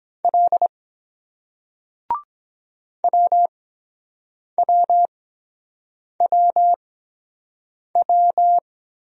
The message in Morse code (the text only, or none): L WWWW